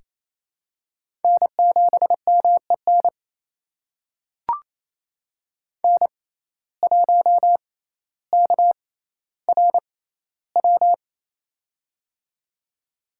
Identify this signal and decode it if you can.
Morse code, heard as a keyed tone, about 29 wpm, spaced as for 15 wpm, 700 Hz. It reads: N7MEN N1KRW
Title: USA Call Signs - 5 Characters - 6x Character Spacing - 28wpm